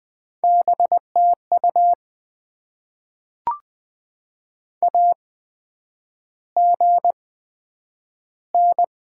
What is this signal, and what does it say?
Morse code, heard as a keyed tone, about 20 wpm, spaced as for 9 wpm, 700 Hz. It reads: BTU AGN
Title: Top 20 QSO Elements - 8x Character Spacing - 20wpm